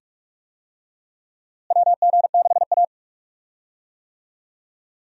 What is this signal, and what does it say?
Morse code, heard as a keyed tone, about 48 wpm, 700 Hz. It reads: WG6A
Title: USA Call Signs - 4 Characters - Repeated 2x - 45wpm